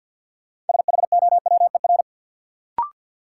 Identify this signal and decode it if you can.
Morse code, heard as a keyed tone, about 52 wpm, 700 Hz. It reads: SHOWER
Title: Sentences from Top 1600 Words - New Word Drill - 50wpm